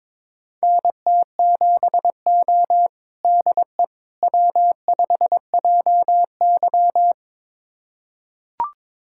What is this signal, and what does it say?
Morse code, heard as a keyed tone, about 22 wpm, 700 Hz. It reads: NT7O DE W5JY
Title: USA Callsign DE Callsign - 4 Characters - 22wpm